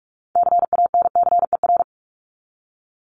Morse code, heard as a keyed tone, about 44 wpm, 700 Hz. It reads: CANCER
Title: Top 1251-1500 Words 45wpm